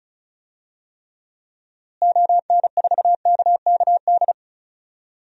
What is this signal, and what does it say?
Morse code, heard as a keyed tone, about 36 wpm, 700 Hz. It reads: ON4KKD
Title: DXCC-100 Call Signs - 6 Characters - Repeated 2x - 35wpm